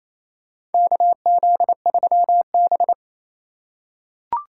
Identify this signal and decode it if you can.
Morse code, heard as a keyed tone, about 28 wpm, 700 Hz. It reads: KZ3B
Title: USA Call Signs - 4 Characters - 2x Character Spacing - 28wpm